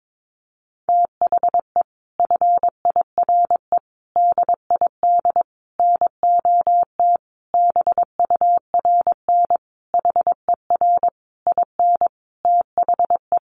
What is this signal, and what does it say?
Morse code, heard as a keyed tone, about 22 wpm, 700 Hz. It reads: THE FIRE DID NOT BURN HER IN THE
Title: Sentences from Top 1300 Words 22wpm